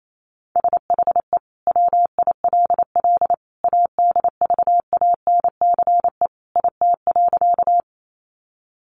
Morse code, heard as a keyed tone, about 28 wpm, 700 Hz. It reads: SHE WILL ADVANCE IT.
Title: Sentences from Top 1400 Words 28wpm